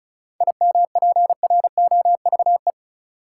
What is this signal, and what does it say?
Morse code, heard as a keyed tone, about 36 wpm, 700 Hz. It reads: IMPROVE
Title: Sentences from Top 800 Words - New Word Drill - 35wpm